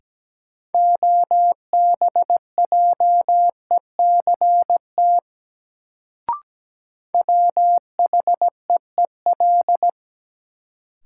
Morse code, heard as a keyed tone, about 17 wpm, 700 Hz. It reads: OBJECT WHEEL
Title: Sentences from Top 500 Words - New Word Drill - 17wpm